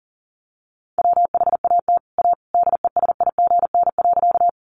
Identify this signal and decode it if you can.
Morse code, heard as a keyed tone, about 40 wpm, 700 Hz. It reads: WHAT A DESIGN.
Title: Sentences from Top 800 Words 40wpm